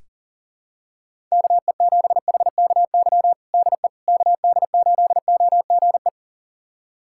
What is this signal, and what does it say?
Morse code, heard as a keyed tone, about 40 wpm, 700 Hz. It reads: KE7HKY DE KD8OGE
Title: USA Callsign DE Callsign - 6 Characters - 40wpm